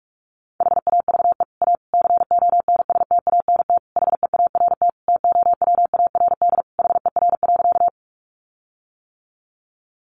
Morse code, heard as a keyed tone, about 44 wpm, 700 Hz. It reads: HAVE A CONSTANT HEART TOWARD HER.
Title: Sentences from Top 1500 Words 45wpm